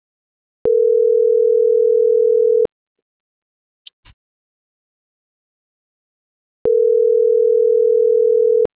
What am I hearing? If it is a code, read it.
Ringback tone (North America), 440 + 480 Hz.